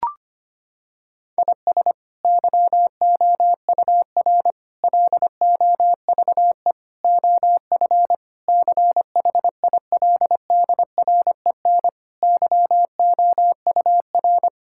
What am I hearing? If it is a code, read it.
Morse code, heard as a keyed tone, about 25 wpm, 700 Hz. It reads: IS YOUR LOVE OF CHILDREN YOUR
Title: Sentences from Top 1000 Words - Review Part 5 - 25wpm